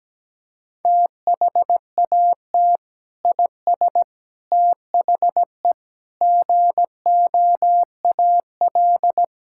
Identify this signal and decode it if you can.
Morse code, heard as a keyed tone, about 17 wpm, 700 Hz. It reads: THAT IS THE GOAL
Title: Sentences from Top 1000 Words - Review Part 5 - 17wpm